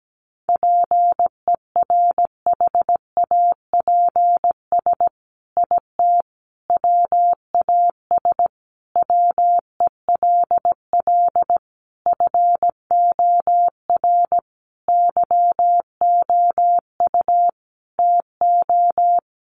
Morse code, heard as a keyed tone, about 17 wpm, 700 Hz. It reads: PERHAPS IT WAS WELL FOR YOU TO
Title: Sentences from Top 1400 Words 17wpm